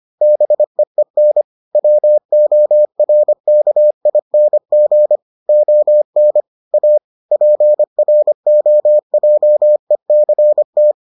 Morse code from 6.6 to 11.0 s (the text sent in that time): A PROJECT